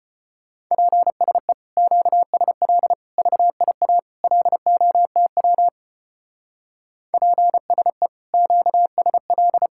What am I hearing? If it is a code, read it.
Morse code, heard as a keyed tone, about 32 wpm, 700 Hz. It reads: PSE QSL VIA LOTW PSE QSL